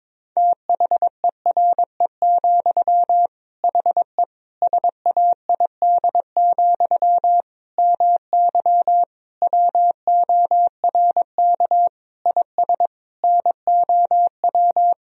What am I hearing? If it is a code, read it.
Morse code, heard as a keyed tone, about 22 wpm, 700 Hz. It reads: THERE, HE SAID, MY WORK IS NOW
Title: Sentences from Top 1000 Words - Review Part 2 - 22wpm